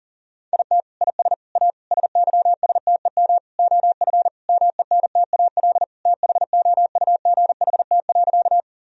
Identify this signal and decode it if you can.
Morse code, heard as a keyed tone, about 37 wpm, 700 Hz. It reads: IT IS A SYSTEM OF MENTAL THOUGHT.
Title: Sentences from Top 1000 Words - Review Part 4 - 40wpm